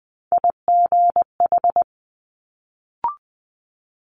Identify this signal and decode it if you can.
Morse code, heard as a keyed tone, about 20 wpm, 700 Hz. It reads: IGH
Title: Multi-letter Phonemes Quick Fire 20wpm